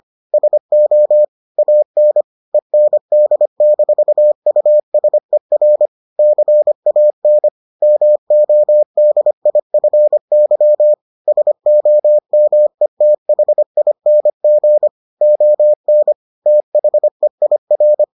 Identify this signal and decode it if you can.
Morse code, heard as a keyed tone, about 25 wpm, 600 Hz. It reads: SO AN END-USER CAN MODIFY SOMETHING ON THEIR